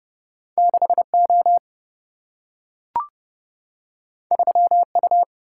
Morse code, heard as a keyed tone, about 30 wpm, 700 Hz. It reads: 6O 3U
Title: Single-Letter Single-Number 30wpm